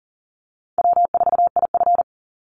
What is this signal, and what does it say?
Morse code, heard as a keyed tone, about 40 wpm, 700 Hz. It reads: W4IF
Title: Easy Call Signs Repeated 3x - 40wpm